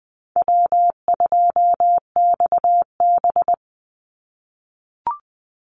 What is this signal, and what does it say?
Morse code, heard as a keyed tone, about 20 wpm, 700 Hz. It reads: W2XB
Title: Easy Call Signs - 20wpm